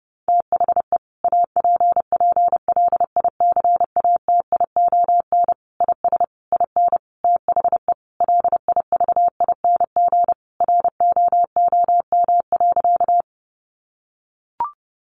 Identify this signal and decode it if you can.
Morse code, heard as a keyed tone, about 30 wpm, 700 Hz. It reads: THE APPLICATION IS IN THE LIVING ROOM.
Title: Sentences from Top 1200 Words 30wpm